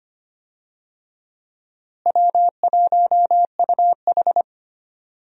Morse code, heard as a keyed tone, about 25 wpm, 700 Hz. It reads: W1UH
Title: USA Call Signs - 4 Characters - Repeated 2x - 25wpm